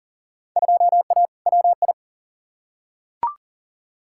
Morse code, heard as a keyed tone, about 40 wpm, 700 Hz. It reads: 2A WI
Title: Field Day Exchanges - 40wpm